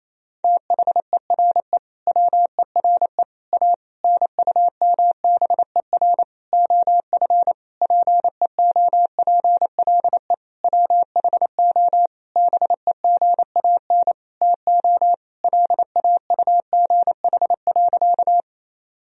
Morse code, heard as a keyed tone, about 28 wpm, 700 Hz. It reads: THERE WERE A NUMBER OF PEOPLE WHO BEGAN TO LAUGH.
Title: Sentences from Top 1000 Words - Review Part 1 - 28wpm